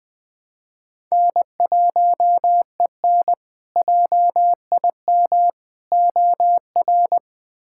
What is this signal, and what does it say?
Morse code, heard as a keyed tone, about 21 wpm, 700 Hz. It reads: N1EN JIM OR
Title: K1USN SST QSO Practice - 20wpm